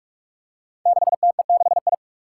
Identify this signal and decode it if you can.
Morse code, heard as a keyed tone, about 45 wpm, 700 Hz. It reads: 6TE6I
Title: Five Challenging Letter-Number Combinations 45wpm